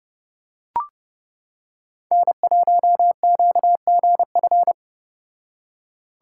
Morse code, heard as a keyed tone, about 29 wpm, 700 Hz. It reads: N1QGF